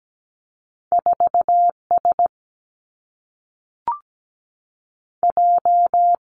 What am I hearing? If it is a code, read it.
Morse code, heard as a keyed tone, about 17 wpm, 700 Hz. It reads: 4 S J
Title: Two Challenging Letter-Number Combinations 17wpm